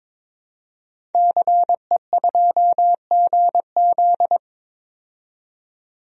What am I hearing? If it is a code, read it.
Morse code, heard as a keyed tone, about 22 wpm, 700 Hz. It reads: CE2GZ
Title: DXCC-100 Call Signs - 5 Characters - 22wpm